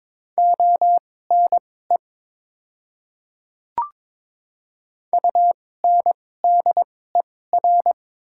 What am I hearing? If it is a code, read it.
Morse code, heard as a keyed tone, about 21 wpm, spaced as for 16 wpm, 700 Hz. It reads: ONE UNDER
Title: 2x Character Spacing - Top 300 Words - 22wpm